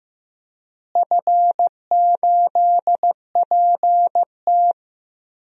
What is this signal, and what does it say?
Morse code, heard as a keyed tone, about 15 wpm, 700 Hz. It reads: F8PT
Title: DXCC-100 Call Signs - 4 Characters - Repeated 3x - 15wpm